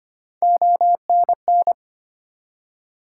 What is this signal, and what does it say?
Morse code, heard as a keyed tone, about 25 wpm, 700 Hz. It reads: ONN
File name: practice / 25wpm / ARRL Sections v2 - 25wpm